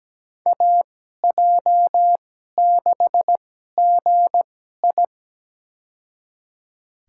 Morse code, heard as a keyed tone, about 17 wpm, spaced as for 12 wpm, 700 Hz. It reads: AJ6GI
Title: USA Call Signs - 5 Characters - 2x Character Spacing - 17wpm